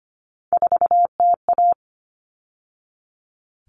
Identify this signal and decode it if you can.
Morse code, heard as a keyed tone, about 25 wpm, 700 Hz. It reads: 4TA
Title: All Cut Numbers - 3-Digit 25wpm